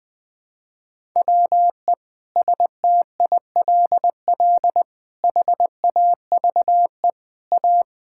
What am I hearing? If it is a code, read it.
Morse code, heard as a keyed tone, about 20 wpm, 700 Hz. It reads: WE STILL HAVE A
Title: Sentences with 6-Word Phrases - 20wpm